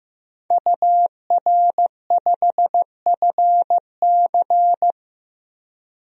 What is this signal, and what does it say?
Morse code, heard as a keyed tone, about 15 wpm, 700 Hz. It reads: UR5FC
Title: DXCC-100 Call Signs - 5 Characters - 15wpm